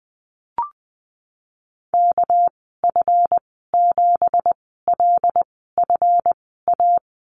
Morse code, heard as a keyed tone, about 20 wpm, spaced as for 15 wpm, 700 Hz. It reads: KF7LFA
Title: 231 Lesson 37g - Letter x - Call Signs 20wpm